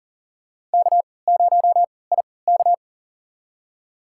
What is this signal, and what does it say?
Morse code, heard as a keyed tone, about 38 wpm, spaced as for 22 wpm, 700 Hz. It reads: K0IK
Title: USA Call Signs - 4 Characters - 3x Character Spacing - 40wpm